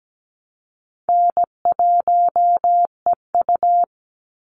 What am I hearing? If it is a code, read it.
Morse code, heard as a keyed tone, about 17 wpm, 700 Hz. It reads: N1EU
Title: NA SOTA Activator Callsigns 17wpm